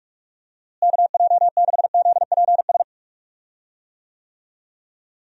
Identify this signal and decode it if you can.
Morse code, heard as a keyed tone, about 46 wpm, 700 Hz. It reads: KJ6ZPS
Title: USA Call Signs - 6 Characters - Repeated 2x - 45wpm